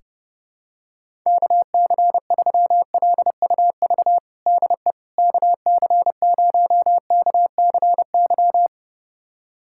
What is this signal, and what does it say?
Morse code, heard as a keyed tone, about 32 wpm, 700 Hz. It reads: KC3LUV DE KC0KCY
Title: USA Callsign DE Callsign 30wpm